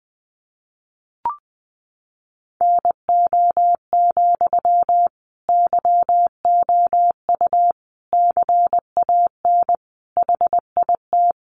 Morse code, heard as a keyed tone, about 20 wpm, 700 Hz. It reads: NO, YOU CAN HIT
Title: Sentences from Top 1200 Words 20wpm